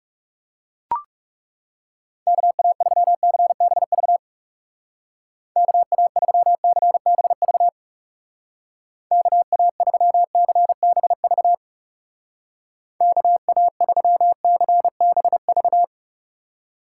Morse code, heard as a keyed tone, about 36 wpm, 700 Hz. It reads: KA3CBV KA3CBV KA3CBV KA3CBV
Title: Speed Racing - USA 2x3 Call Signs 30wpm